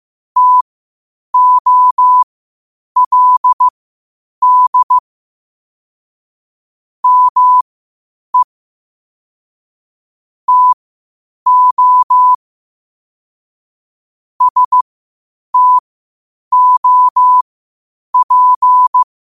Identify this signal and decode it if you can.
Morse code, heard as a keyed tone, about 15 wpm, spaced as for 8 wpm, 1000 Hz. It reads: TOLD ME TO STOP
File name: 8wpm